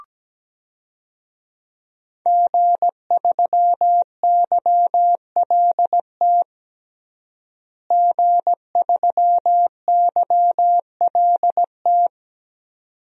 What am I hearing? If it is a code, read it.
Morse code, heard as a keyed tone, about 17 wpm, 700 Hz. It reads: G3YLT G3YLT
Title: DXCC-100 Call Signs - 5 Characters - Repeated 2x - 17wpm